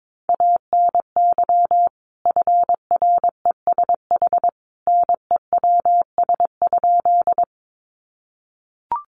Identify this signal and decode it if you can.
Morse code, heard as a keyed tone, about 22 wpm, 700 Hz. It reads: ANY FRESH NEWS?
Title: Sentences from Top 1200 Words 22wpm